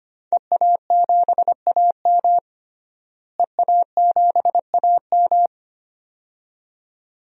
Morse code, heard as a keyed tone, about 25 wpm, 700 Hz. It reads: EA7AM EA7AM
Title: DXCC-100 Call Signs - 5 Characters - Repeated 2x - 25wpm